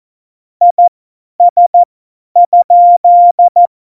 Morse code, heard as a keyed tone, about 14 wpm, spaced as for 10 wpm, 700 Hz. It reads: IS?
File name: Morsekurs / Witze / Witz 03. November 2020, 50 F70